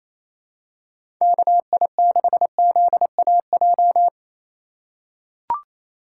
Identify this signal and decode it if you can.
Morse code, heard as a keyed tone, about 27 wpm, 700 Hz. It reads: KI6ZAJ
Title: USA Call Signs - 6 Characters - 4x Character Spacing - 28wpm